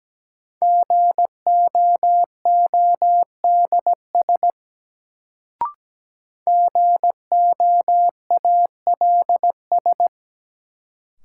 Morse code, heard as a keyed tone, about 17 wpm, 700 Hz. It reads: GOODS GOALS
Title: Sentences from Top 1000 Words - Review Part 6 - New Word Drill - 17wpm